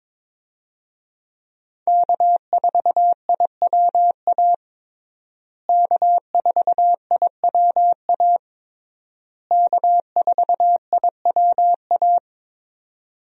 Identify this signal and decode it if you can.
Morse code, heard as a keyed tone, about 22 wpm, 700 Hz. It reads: K4IWA K4IWA K4IWA
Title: USA Call Signs - 5 Characters - Repeated 3x - 22wpm